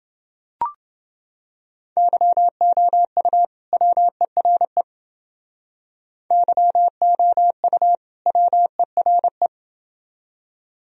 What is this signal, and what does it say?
Morse code, heard as a keyed tone, about 28 wpm, 700 Hz. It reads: YOU WERE YOU WERE